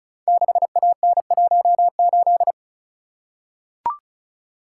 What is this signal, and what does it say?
Morse code, heard as a keyed tone, about 36 wpm, 700 Hz. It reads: 6AN18
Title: Code Groups - 5 Letter:Numbers - 35wpm